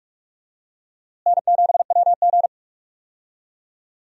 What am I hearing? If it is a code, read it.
Morse code, heard as a keyed tone, about 45 wpm, 700 Hz. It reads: N7WG
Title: USA Call Signs - 4 Characters - Repeated 2x - 45wpm